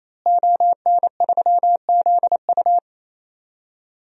Morse code, heard as a keyed tone, about 28 wpm, 700 Hz. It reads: ON3ZU
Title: DXCC-100 Call Signs - 5 Characters - Repeated 3x - 28wpm